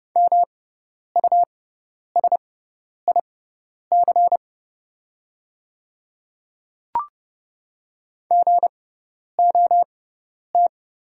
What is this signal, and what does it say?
Morse code, heard as a keyed tone, about 29 wpm, spaced as for 10 wpm, 690 Hz. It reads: MUSIC GOT
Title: 6x Character Spacing - Top 300 Words - 30wpm